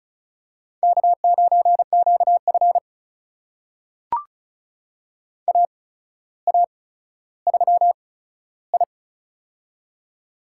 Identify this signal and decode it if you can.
Morse code, heard as a keyed tone, about 36 wpm, spaced as for 15 wpm, 700 Hz. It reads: K9QF AA3I
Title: USA Call Signs - 4 Characters - 8x Character Spacing - 35wpm